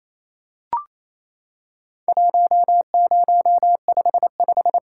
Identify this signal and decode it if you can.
Morse code, heard as a keyed tone, about 28 wpm, 700 Hz. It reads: 1055
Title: Worked All Europe DX Contest - QTC Exchange - 28wpm